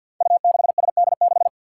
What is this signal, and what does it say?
Morse code, heard as a keyed tone, about 52 wpm, 700 Hz. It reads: U6SD6
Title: Five Challenging Letter-Number Combinations 50wpm